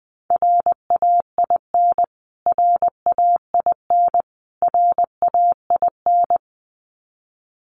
Morse code, heard as a keyed tone, about 20 wpm, 700 Hz. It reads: RAIN RAIN RAIN
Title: Top 500 Words Repeated 20wpm